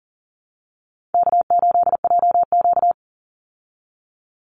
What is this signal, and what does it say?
Morse code, heard as a keyed tone, about 40 wpm, 700 Hz. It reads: K8JQ
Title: Call Signs Repeated 3x - 40wpm